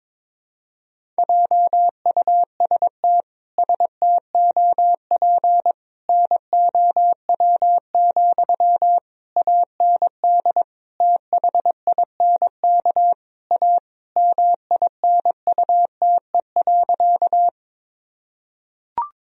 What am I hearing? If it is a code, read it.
Morse code, heard as a keyed tone, about 22 wpm, 700 Hz. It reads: JUST STOP NOW, AND THINK A MINUTE.
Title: Sentences from Top 1000 Words - Review Part 1 - 22wpm